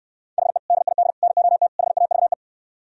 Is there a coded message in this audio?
Yes: Morse code from 0.4 to 2.3 s, heard as a keyed tone, about 73 wpm, 700 Hz. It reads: HE DID NOT HAVE